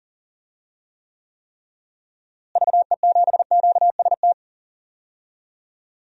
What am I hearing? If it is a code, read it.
Morse code, heard as a keyed tone, about 40 wpm, 700 Hz. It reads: VE7QST
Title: DXCC-100 Call Signs - 6 Characters - Repeated 3x - 40wpm